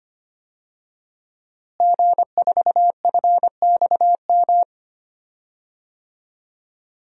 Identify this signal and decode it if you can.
Morse code, heard as a keyed tone, about 25 wpm, 700 Hz. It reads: G4FXM
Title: DXCC-100 Call Signs - 5 Characters - Repeated 2x - 25wpm